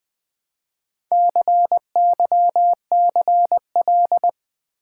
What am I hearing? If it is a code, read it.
Morse code, heard as a keyed tone, about 20 wpm, 700 Hz. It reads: CYCL
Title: Build A Word - 1 Letter At A Time - 5-Letter Words - 20wpm